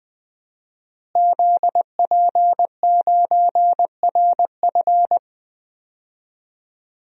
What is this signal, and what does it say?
Morse code, heard as a keyed tone, about 20 wpm, 700 Hz. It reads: ZP9RF
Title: DXCC-100 Call Signs - 5 Characters - Repeated 3x - 20wpm